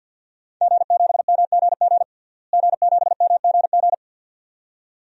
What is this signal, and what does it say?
Morse code, heard as a keyed tone, about 50 wpm, 700 Hz. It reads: G7MGG G7MGG
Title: DXCC-100 Call Signs - 5 Characters - Repeated 2x - 50wpm